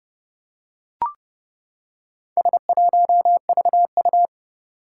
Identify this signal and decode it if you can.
Morse code, heard as a keyed tone, about 30 wpm, 700 Hz. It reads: S1VU
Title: Four Challenging Letter-Number Combinations 30wpm